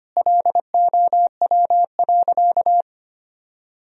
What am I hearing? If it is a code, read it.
Morse code, heard as a keyed tone, about 25 wpm, 700 Hz. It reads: LOW.
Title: Sentences from Top 1000 Words - Review Part 1 - 25wpm